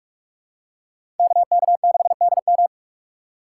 Morse code, heard as a keyed tone, about 46 wpm, 700 Hz. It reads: KK6DM
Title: USA Call Signs - 5 Characters - 2x Character Spacing - 45wpm